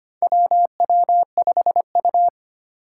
Morse code, heard as a keyed tone, about 25 wpm, 700 Hz. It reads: WW5U
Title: USA Call Signs - 4 Characters - 2x Character Spacing - 25wpm